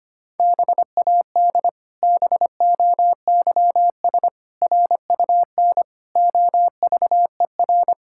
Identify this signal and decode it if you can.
Morse code, heard as a keyed tone, about 25 wpm, 700 Hz. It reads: BAD BOYS RUN OVER
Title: Pangram Challenge v2 - 25wpm